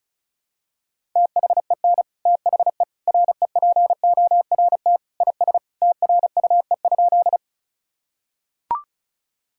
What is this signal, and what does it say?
Morse code, heard as a keyed tone, about 34 wpm, 700 Hz. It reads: THEN THE REPORT IS TRUE?
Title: Sentences from Top 1000 Words - Review Part 6 - 35wpm